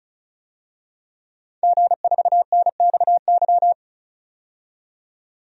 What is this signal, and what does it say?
Morse code, heard as a keyed tone, about 36 wpm, 700 Hz. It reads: G4NXY
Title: DXCC-100 Call Signs - 5 Characters - 35wpm